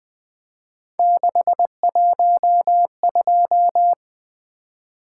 Morse code, heard as a keyed tone, about 20 wpm, 700 Hz. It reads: 612
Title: Three Letter-Number Combinations - 20wpm